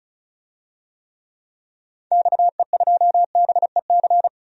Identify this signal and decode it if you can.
Morse code, heard as a keyed tone, about 34 wpm, 700 Hz. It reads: XE2BEC
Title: DXCC-100 Call Signs - 6 Characters - 35wpm